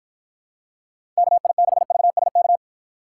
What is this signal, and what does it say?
Morse code, heard as a keyed tone, about 49 wpm, 700 Hz. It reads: KI6VSK